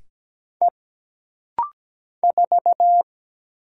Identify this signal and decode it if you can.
Morse code, heard as a keyed tone, about 17 wpm, 700 Hz. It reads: E 4
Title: Single Letter-Number - Rapid-Fire 17wpm